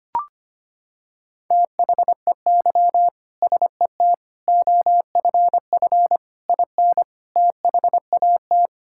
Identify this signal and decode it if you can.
Morse code, heard as a keyed tone, about 25 wpm, 700 Hz. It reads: THEY SET OFF IN THAT
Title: Sentences from Top 1000 Words - Review Part 5 - 25wpm